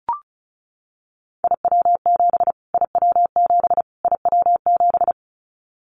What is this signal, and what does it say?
Morse code, heard as a keyed tone, about 34 wpm, 700 Hz. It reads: IW7 IW7 IW7
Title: Callsign Prefixes for Top 25 DXCC - Repeated 3x - 35wpm